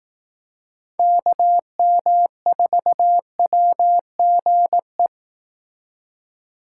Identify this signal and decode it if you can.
Morse code, heard as a keyed tone, about 18 wpm, 700 Hz. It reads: KM4WGE
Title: Speed Racing - USA 2x3 Call Signs 15wpm